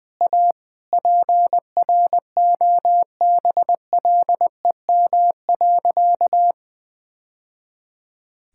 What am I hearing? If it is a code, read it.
Morse code, heard as a keyed tone, about 20 wpm, 700 Hz. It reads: A PROBLEM.
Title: Sentences with 2-Word Phrases - Spoken Beginning - 20wpm